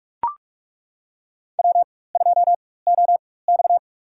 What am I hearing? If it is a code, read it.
Morse code, heard as a keyed tone, about 42 wpm, spaced as for 20 wpm, 700 Hz. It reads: W2OX
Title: USA Call Signs - 4 Characters - 4x Character Spacing - 45wpm